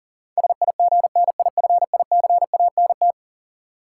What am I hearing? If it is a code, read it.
Morse code, heard as a keyed tone, about 40 wpm, 700 Hz. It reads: SIGNIFICANT
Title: Sentences from Top 1000 Words - Review Part 6 - New Word Drill - 40wpm